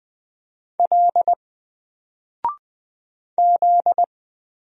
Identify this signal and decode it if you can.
Morse code, heard as a keyed tone, about 20 wpm, 700 Hz. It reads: L Z
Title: Single Letter-Number 20wpm